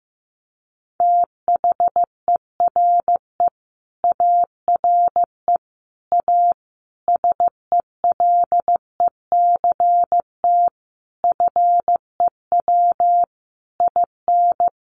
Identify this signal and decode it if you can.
Morse code, heard as a keyed tone, about 15 wpm, 700 Hz. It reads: THERE ARE A SELECT FEW IN
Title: Sentences from Top 1300 Words 15wpm